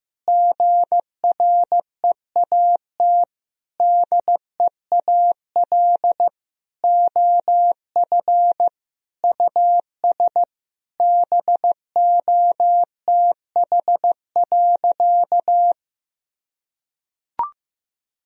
Morse code, heard as a keyed tone, about 16 wpm, 700 Hz. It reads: GREAT DEAL OF US BOTH.
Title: Sentences from Top 1000 Words - Review Part 4 - 15wpm